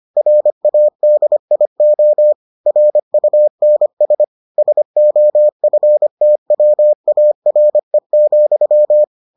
Morse code, heard as a keyed tone, about 25 wpm, 600 Hz. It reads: RADIO RUNS SOFTWARE,